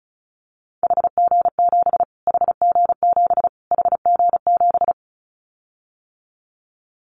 Morse code, heard as a keyed tone, about 34 wpm, 700 Hz. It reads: HG7 HG7 HG7
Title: Callsign Prefixes for Top 25 DXCC - Repeated 3x - 35wpm